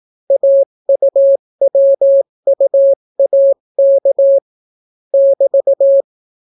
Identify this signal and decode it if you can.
Morse code, heard as a keyed tone, about 18 wpm, 550 Hz. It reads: AUWUAK <BT>